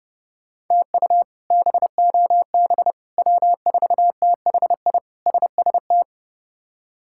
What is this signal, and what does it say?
Morse code, heard as a keyed tone, about 30 wpm, 700 Hz. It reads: TU BOB W4THI SST
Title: K1USN SST QSO Practice - 30wpm